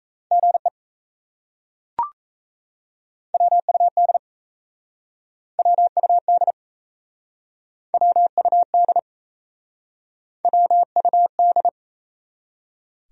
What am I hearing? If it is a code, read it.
Morse code, heard as a keyed tone, about 37 wpm, 700 Hz. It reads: GE WUD WUD WUD WUD